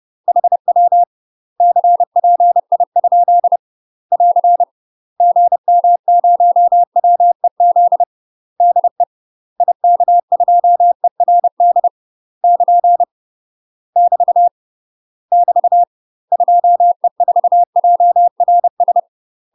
Morse code, heard as a keyed tone, about 30 wpm, 700 Hz. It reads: HW CPI? <AR> GM0WEZ DE IK2ERD <KN> <BT> <BT> 2E4JRS
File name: UK MASTER QSOs TRAINING / 30WPM / UK MASTER QSOs 0005